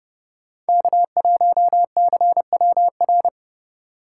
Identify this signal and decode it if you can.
Morse code, heard as a keyed tone, about 34 wpm, 700 Hz. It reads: K1CWR